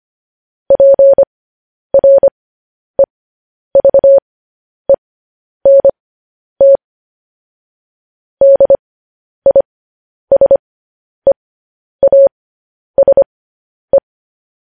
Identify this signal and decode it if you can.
Morse code, heard as a keyed tone, about 25 wpm, spaced as for 10 wpm, 550 Hz. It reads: PREVENT DISEASE
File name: podcast / News headlines-1x-25x10